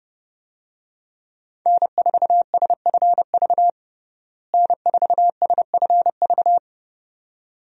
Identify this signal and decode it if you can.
Morse code, heard as a keyed tone, about 31 wpm, 700 Hz. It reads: N4SFV N4SFV
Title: USA Call Signs - 5 Characters - Repeated 2x - 30wpm